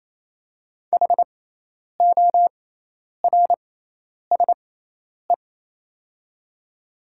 Morse code, heard as a keyed tone, about 28 wpm, spaced as for 10 wpm, 700 Hz. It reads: HORSE
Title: 6x Character Spacing - Top 300 Words - 28wpm